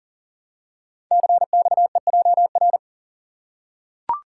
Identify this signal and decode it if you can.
Morse code, heard as a keyed tone, about 39 wpm, 690 Hz. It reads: CXEJR
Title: Code Groups - 5 Letter:Numbers - 40wpm